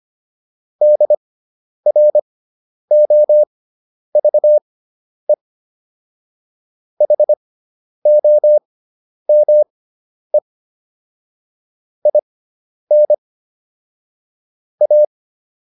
Morse code, heard as a keyed tone, about 25 wpm, spaced as for 10 wpm, 600 Hz. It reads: DROVE HOME IN A